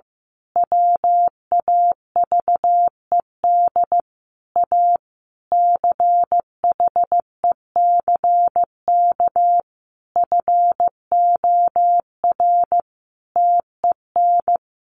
Morse code, heard as a keyed tone, about 15 wpm, 700 Hz. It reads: WAVED A CHECK FOR TEN
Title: A Sound Of Thunder 15wpm